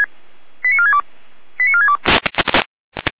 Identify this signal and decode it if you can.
Five-tone selective call (ZVEI), heard as quick runs of five tones, tones of about 70 ms. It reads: 89462 89462
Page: SDR/ZVEI